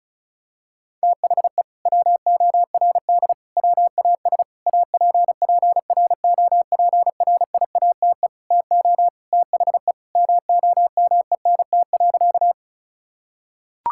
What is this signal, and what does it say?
Morse code, heard as a keyed tone, about 36 wpm, 700 Hz. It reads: THE WORD WAS APPROPRIATE TO THE MOMENT.
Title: Sentences from Top 1000 Words - Review Part 5 - 35wpm